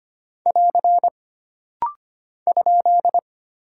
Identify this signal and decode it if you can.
Morse code, heard as a keyed tone, about 25 wpm, 700 Hz. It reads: <AR> ?
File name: Single Punctuation-Prosign - Rapid-Fire - 25wpm